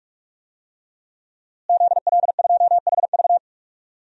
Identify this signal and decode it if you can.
Morse code, heard as a keyed tone, about 46 wpm, 700 Hz. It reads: ZL2HV
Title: DXCC-100 Call Signs - 5 Characters - Repeated 3x - 45wpm